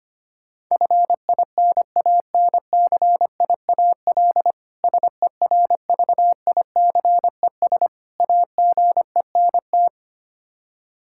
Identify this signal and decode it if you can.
Morse code, heard as a keyed tone, about 25 wpm, 700 Hz. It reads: FINANCIAL SERVICES AGENT
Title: Practice QSO - Exchange 3 - More Info - 25wpm